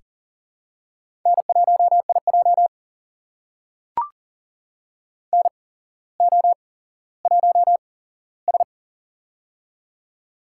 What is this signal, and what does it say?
Morse code, heard as a keyed tone, about 37 wpm, spaced as for 17 wpm, 700 Hz. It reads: N1IJ NO1S
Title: USA Call Signs - 4 Characters - 8x Character Spacing - 40wpm